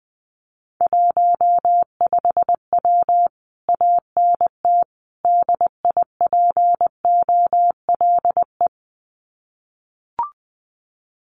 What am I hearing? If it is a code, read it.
Morse code, heard as a keyed tone, about 20 wpm, 700 Hz. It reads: 15W ANT DIPOLE
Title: Common QSO Phrases Encore 20wpm